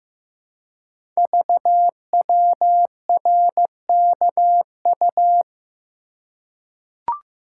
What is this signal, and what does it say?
Morse code, heard as a keyed tone, about 15 wpm, 700 Hz. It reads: VWRKU
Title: Code Groups - 5 Letter:Numbers - 15wpm